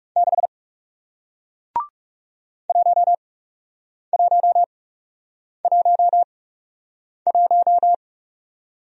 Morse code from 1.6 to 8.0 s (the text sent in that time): 1111